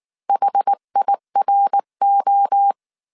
Morse code, heard as a keyed tone, about 19 wpm, 800 Hz. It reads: HIRO